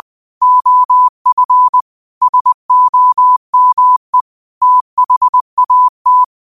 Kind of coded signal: Morse code